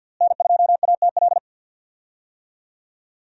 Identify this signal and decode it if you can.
Morse code, heard as a keyed tone, about 52 wpm, 700 Hz. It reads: N2ATL
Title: USA Call Signs - 5 Characters - Repeated 3x - 50wpm